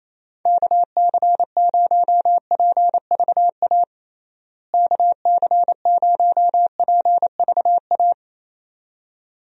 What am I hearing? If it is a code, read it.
Morse code, heard as a keyed tone, about 28 wpm, 700 Hz. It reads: KC0PVA KC0PVA
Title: USA Call Signs - 6 Characters - Repeated 2x - 28wpm